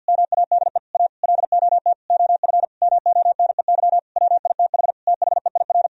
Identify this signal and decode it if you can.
Morse code, heard as a keyed tone, about 50 wpm, 700 Hz. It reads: MADE A LOT OF MONEY WITH THEIR
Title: Sentences from Top 1000 Words - Review Part 4 - 50wpm